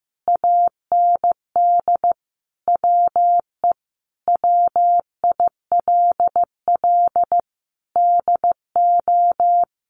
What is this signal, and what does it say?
Morse code, heard as a keyed tone, about 15 wpm, 700 Hz. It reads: AND WE WILL DO
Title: Sentences from Top 900 Words 15wpm